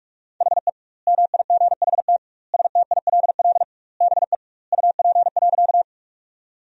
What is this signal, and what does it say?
Morse code, heard as a keyed tone, about 46 wpm, 700 Hz. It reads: HE MIGHT STILL BE UP.
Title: 2x Word-Spacing - Sentences from Top 300 Words - 45wpm